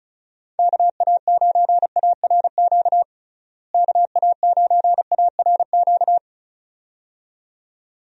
Morse code, heard as a keyed tone, about 36 wpm, 700 Hz. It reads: KA9ARQ KA9ARQ
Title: USA Call Signs - 6 Characters - Repeated 2x - 35wpm